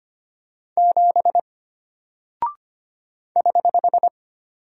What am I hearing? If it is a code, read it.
Morse code, heard as a keyed tone, about 25 wpm, 700 Hz. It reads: 7 <HH>
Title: Single Character (Letter:Number:Punctuation:Prosign) Alt 25wpm